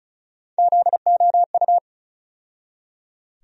Morse code, heard as a keyed tone, about 36 wpm, 700 Hz. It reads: ZOU
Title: USA Call Sign Postfixes - 3 Letters 35wpm